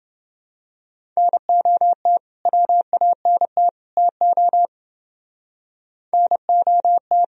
Morse code, heard as a keyed tone, about 29 wpm, 700 Hz. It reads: NOT WANT TO NOT